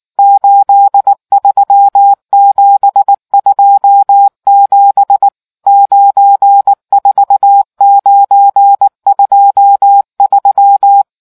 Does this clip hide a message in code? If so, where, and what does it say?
Yes: Morse code from 0.2 to 11.0 s, heard as a keyed tone, about 19 wpm, 800 Hz. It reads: 83727 94923